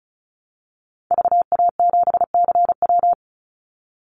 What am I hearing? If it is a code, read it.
Morse code, heard as a keyed tone, about 36 wpm, 700 Hz. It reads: VA7CW
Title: Call Signs - 35wpm